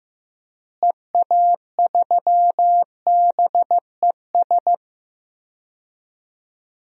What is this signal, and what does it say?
Morse code, heard as a keyed tone, about 15 wpm, 700 Hz. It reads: EA3BES
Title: DXCC-100 Call Signs - 6 Characters - Repeated 3x - 15wpm